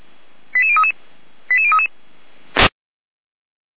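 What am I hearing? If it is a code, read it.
Five-tone selective call (ZVEI), heard as quick runs of five tones, tones of about 70 ms. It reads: 88033 88033
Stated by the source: ZVEI five-tone selective call received on an SDR.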